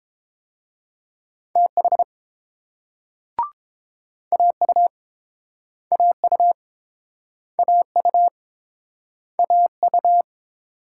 Morse code, heard as a keyed tone, about 30 wpm, 700 Hz. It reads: TH AU AU AU AU